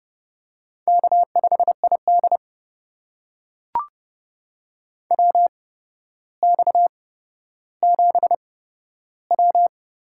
Morse code, heard as a keyed tone, about 30 wpm, spaced as for 13 wpm, 700 Hz. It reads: K5ID WX7W
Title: USA Call Signs - 4 Characters - 8x Character Spacing - 30wpm